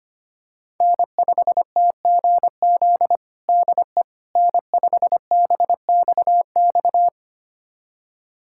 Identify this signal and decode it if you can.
Morse code, heard as a keyed tone, about 25 wpm, 700 Hz. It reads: N5TGZ DE N5BXX
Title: USA Callsign DE Callsign - 5 Characters - 25wpm